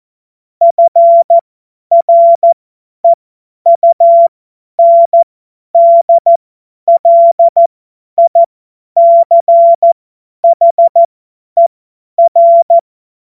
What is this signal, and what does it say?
Morse code, heard as a keyed tone, about 14 wpm, spaced as for 10 wpm, 680 Hz. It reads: FREUNDLICHER
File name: Witz-01-Juli-2025-50-F70